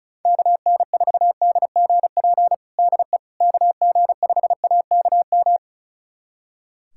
Morse code, heard as a keyed tone, about 36 wpm, 700 Hz. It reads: KN4DGP DE KG5AKM
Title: USA Callsign DE Callsign - 6 Characters - 35wpm